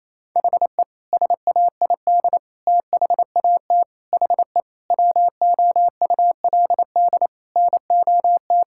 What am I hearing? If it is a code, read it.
Morse code, heard as a keyed tone, about 27 wpm, 700 Hz. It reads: HE SAID THAT HE WOULD NOT
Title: Sentences from Top 1000 Words - Review Part 6 - 28wpm